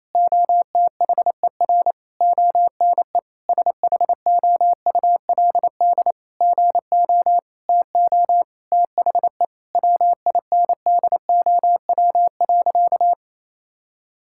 Morse code, heard as a keyed tone, about 28 wpm, 700 Hz. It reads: OTHER ONE SHOULD GO TO THE WINDOW.
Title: Sentences from Top 1600 Words 28wpm